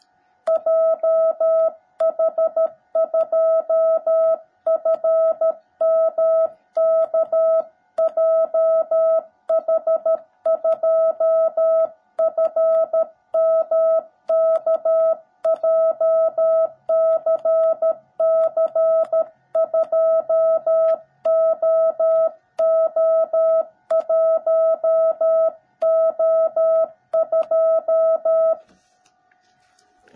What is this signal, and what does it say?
Morse code, heard as a keyed tone, about 13 wpm, 660 Hz. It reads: JH2FMK JH2FMK JCC2OO1O2
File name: jh8cbh / 2021cw / cw17b